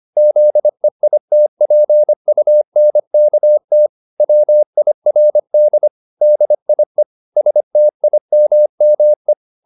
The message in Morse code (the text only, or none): ZEITPUNKT WIRD DIE STIMME